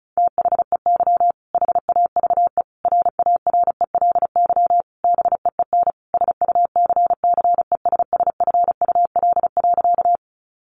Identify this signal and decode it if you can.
Morse code, heard as a keyed tone, about 34 wpm, 700 Hz. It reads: THEY HAVE RARELY BEEN SUCCESSFUL.
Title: Sentences from Top 1400 Words 35wpm